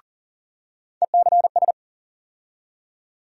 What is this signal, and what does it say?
Morse code, heard as a keyed tone, about 40 wpm, 700 Hz. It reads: ECS
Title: USA Call Sign Postfixes - 3 Letters 40wpm